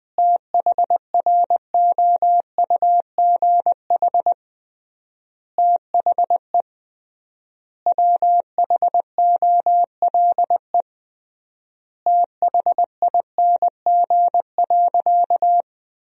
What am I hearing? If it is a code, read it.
Morse code, heard as a keyed tone, about 21 wpm, 700 Hz. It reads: THROUGH THE WHOLE THING.
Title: 3x Word-Spacing - Sentences from Top 400 Words - 20wpm